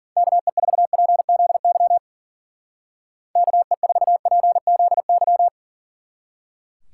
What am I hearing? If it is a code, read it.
Morse code, heard as a keyed tone, about 48 wpm, 700 Hz. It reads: KE4PZY KE4PZY
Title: Speed Racing - USA 2x3 Call Signs 40wpm